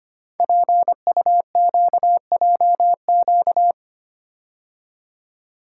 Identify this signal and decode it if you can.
Morse code, heard as a keyed tone, about 25 wpm, 700 Hz. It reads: PUQJQ
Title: Code Groups - 5 Letters - 25wpm